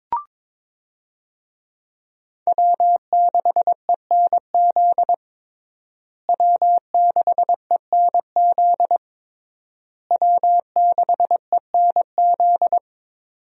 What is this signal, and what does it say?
Morse code, heard as a keyed tone, about 22 wpm, 700 Hz. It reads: W6ENZ W6ENZ W6ENZ
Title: USA Call Signs - 5 Characters - Repeated 3x - 22wpm